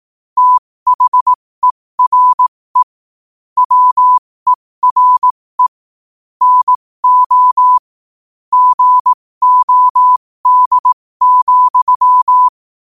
Morse code, heard as a keyed tone, about 18 wpm, spaced as for 15 wpm, 1000 Hz. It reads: THERE WERE NO GOD,
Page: 15 wpm morse code quotes